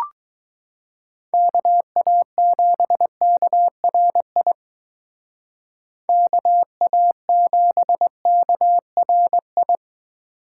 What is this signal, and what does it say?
Morse code, heard as a keyed tone, about 19 wpm, 700 Hz. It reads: KA7KRI KA7KRI
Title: Speed Racing - USA Call Signs 15wpm